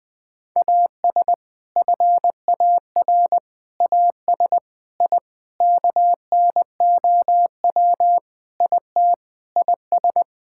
Morse code, heard as a keyed tone, about 20 wpm, 700 Hz. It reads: AS FAR AS I KNOW IT IS
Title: Sentences with 4-Word Phrases - Spoken Beginning - 20wpm